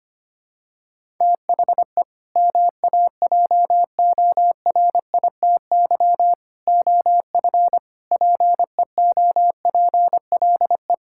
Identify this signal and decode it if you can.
Morse code, heard as a keyed tone, about 25 wpm, 700 Hz. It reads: THE MAJORITY OF PEOPLE
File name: practice / 25wpm / Sentences with 3-Word Phrases - 25wpm